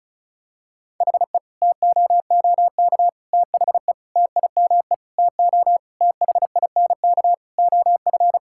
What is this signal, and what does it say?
Morse code, heard as a keyed tone, about 34 wpm, 700 Hz. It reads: HE TOOK THE TIME TO THINK OF
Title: Sentences from Top 1600 Words 35wpm